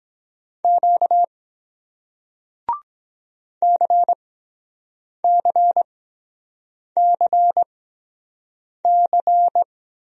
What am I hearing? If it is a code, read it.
Morse code, heard as a keyed tone, about 23 wpm, spaced as for 7 wpm, 700 Hz. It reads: Q CCCC